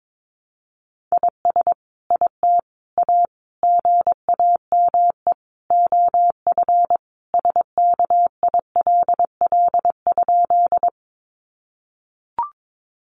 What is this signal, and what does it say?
Morse code, heard as a keyed tone, about 22 wpm, 700 Hz. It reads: IS IT A GAME OF SKILL?
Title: Sentences from Top 1200 Words 22wpm